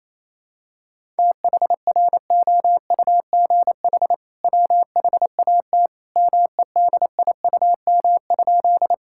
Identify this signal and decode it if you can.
Morse code, heard as a keyed tone, about 27 wpm, 700 Hz. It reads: THROUGH WHAT MEDIUM?
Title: Sentences from Top 1000 Words - Review Part 6 - 28wpm